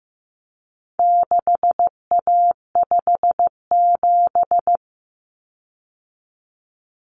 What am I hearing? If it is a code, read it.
Morse code, heard as a keyed tone, about 15 wpm, 700 Hz. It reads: 6A57
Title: Common Cut Numbers - 4-Digit 15wpm